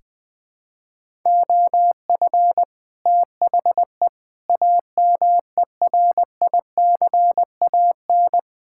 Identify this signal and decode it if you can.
Morse code, heard as a keyed tone, about 20 wpm, 700 Hz. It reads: OF THE AMERICAN
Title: Build A Phrase - 3-Word Phrases - 20wpm